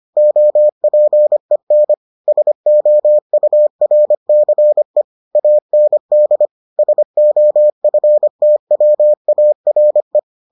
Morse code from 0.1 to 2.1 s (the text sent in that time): OPEN